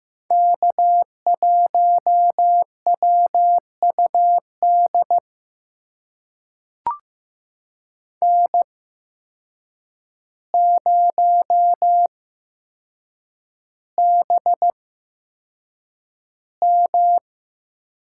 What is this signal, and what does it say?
Morse code, heard as a keyed tone, about 15 wpm, spaced as for 7 wpm, 700 Hz. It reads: K1WUD N0BM
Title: USA Call Signs - 8x Character Spacing - 15wpm